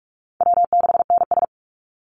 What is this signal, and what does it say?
Morse code, heard as a keyed tone, about 44 wpm, 700 Hz. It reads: W6NS
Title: Easy Call Signs - 45wpm